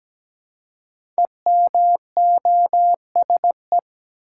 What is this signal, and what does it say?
Morse code, heard as a keyed tone, about 17 wpm, 700 Hz. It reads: EMOSE
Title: Code Groups - 5 Letters - 17wpm